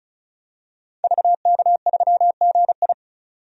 Morse code, heard as a keyed tone, about 34 wpm, 700 Hz. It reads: VK3GI
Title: DXCC-100 Call Signs - 5 Characters - 35wpm